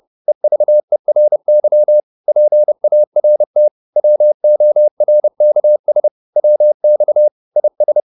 Morse code, heard as a keyed tone, about 30 wpm, 600 Hz. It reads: EVERY PART WORKS WX IS